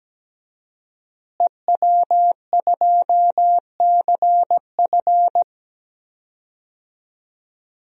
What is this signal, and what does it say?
Morse code, heard as a keyed tone, about 17 wpm, 700 Hz. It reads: EW2CF
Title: DXCC-100 Call Signs - 5 Characters - Repeated 3x - 17wpm